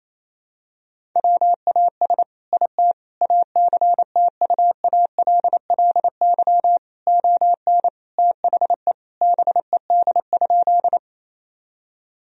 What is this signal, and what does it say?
Morse code, heard as a keyed tone, about 28 wpm, 700 Hz. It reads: WAS IT ACTUALLY ON THE BED?
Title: Sentences from Top 1000 Words - Review Part 4 - 28wpm